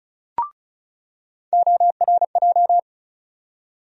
Morse code, heard as a keyed tone, about 36 wpm, 700 Hz. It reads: ORJ
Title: USA Call Sign Postfixes - 3 Letters 35wpm